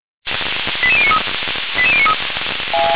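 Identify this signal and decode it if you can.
Five-tone selective call (ZVEI), heard as quick runs of five tones, tones of about 70 ms. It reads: 88093 88093
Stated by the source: ZVEI five-tone selective call received by SDR, from fire brigade Feuerwehr Orlamünde.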